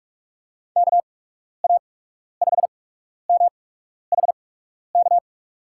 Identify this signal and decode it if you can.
Morse code, heard as a keyed tone, about 43 wpm, spaced as for 12 wpm, 700 Hz. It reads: KA5MHK
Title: USA Call Signs - 6 Characters - 8x Character Spacing - 45wpm